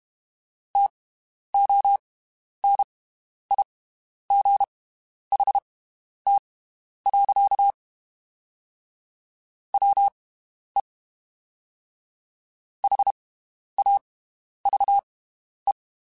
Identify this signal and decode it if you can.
Morse code, heard as a keyed tone, about 32 wpm, spaced as for 11 wpm, 800 Hz. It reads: TONIGHT. WE HAVE
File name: (spaced) / QSOs Spaced-mp3s / WD0BGP 1 @30 sp